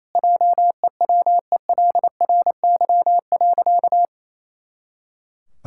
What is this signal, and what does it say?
Morse code, heard as a keyed tone, about 27 wpm, 700 Hz. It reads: JEWELRY.
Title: Pangram Challenge v2 - 28wpm